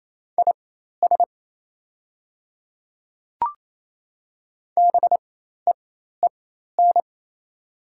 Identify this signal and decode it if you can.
Morse code, heard as a keyed tone, about 27 wpm, spaced as for 13 wpm, 690 Hz. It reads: IS BEEN
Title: 4x Character Spacing - Top 200 Words - 28wpm